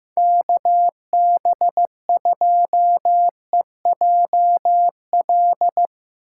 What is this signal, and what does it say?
Morse code, heard as a keyed tone, about 16 wpm, 700 Hz. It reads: KB2EJL
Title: Speed Racing - USA 2x3 Call Signs 15wpm